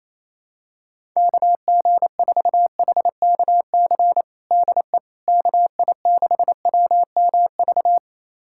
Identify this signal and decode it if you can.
Morse code, heard as a keyed tone, about 28 wpm, 700 Hz. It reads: KG4HKC DE KI6WMV
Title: USA Callsign DE Callsign 28wpm